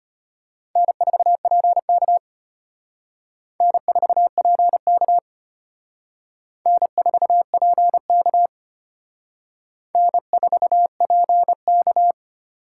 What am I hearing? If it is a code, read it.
Morse code, heard as a keyed tone, about 31 wpm, 700 Hz. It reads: N4PK N4PK N4PK N4PK
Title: Speed Racing - USA 1x2 Call Signs 25wpm